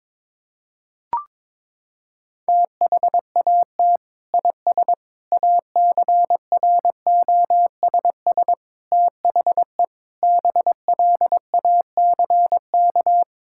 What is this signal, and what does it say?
Morse code, heard as a keyed tone, about 22 wpm, 700 Hz. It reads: THAT IS ACROSS THE BLACK
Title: Sentences from Top 1000 Words - Review Part 5 - 22wpm